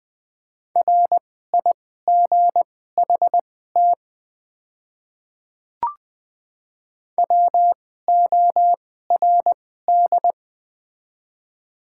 Morse code, heard as a keyed tone, about 20 wpm, spaced as for 15 wpm, 700 Hz. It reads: RIGHT WORD
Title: 2x Character Spacing - Top 200 Words - 20wpm